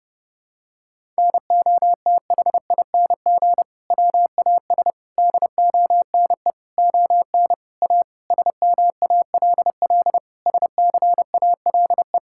Morse code, heard as a keyed tone, about 30 wpm, 700 Hz. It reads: NOTHING WAS DONE ON A SMALL SCALE
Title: Sentences from Top 1000 Words - Review Part 5 - 30wpm